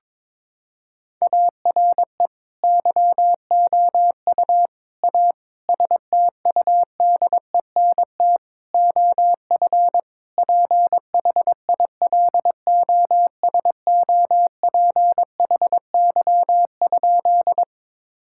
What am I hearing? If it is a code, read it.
Morse code, heard as a keyed tone, about 21 wpm, 700 Hz. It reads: ARE YOU A STUDENT OF PHILOSOPHY?
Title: Sentences from Top 1000 Words - Review Part 6 - 22wpm